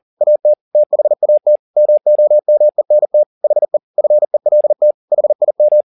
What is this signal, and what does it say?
Morse code, heard as a keyed tone, about 40 wpm, 600 Hz. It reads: AT THAT MOMENT HE FELT HIM